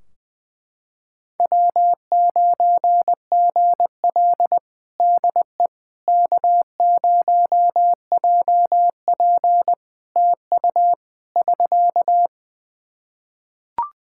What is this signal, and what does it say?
Morse code, heard as a keyed tone, about 20 wpm, 700 Hz. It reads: W9GL DE K0JP TU <SK>
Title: Practice QSO - Exchange 4 - Ending - 20wpm